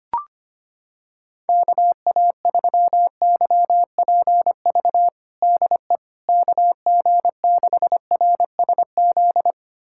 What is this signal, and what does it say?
Morse code, heard as a keyed tone, about 25 wpm, 700 Hz. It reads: KA3YPV DE KG6RSZ
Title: USA Callsign DE Callsign 25wpm